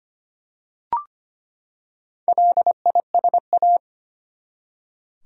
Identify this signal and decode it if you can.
Morse code, heard as a keyed tone, about 25 wpm, 700 Hz. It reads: LISA
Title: USA Common Names - Max 4 Letters - 25wpm